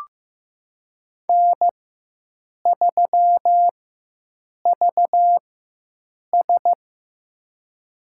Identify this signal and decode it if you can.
Morse code, heard as a keyed tone, about 15 wpm, spaced as for 7 wpm, 700 Hz. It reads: N3VS
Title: USA Call Signs - 4 Characters - 4x Character Spacing - 15wpm